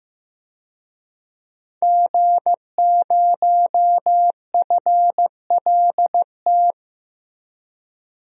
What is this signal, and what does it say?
Morse code, heard as a keyed tone, about 15 wpm, 700 Hz. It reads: G0FLT
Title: DXCC-100 Call Signs - 5 Characters - Repeated 2x - 15wpm